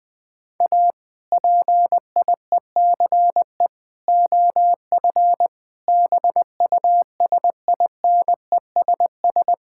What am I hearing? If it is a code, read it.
Morse code, heard as a keyed tone, about 20 wpm, 700 Hz. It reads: A PIECE OF BUSINESS
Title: Sentences from Top 1000 Words - Review Part 5 - 20wpm